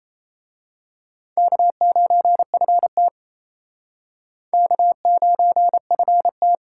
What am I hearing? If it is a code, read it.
Morse code, heard as a keyed tone, about 30 wpm, 700 Hz. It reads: K9FT K9FT
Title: Speed Racing - USA Call Signs 28wpm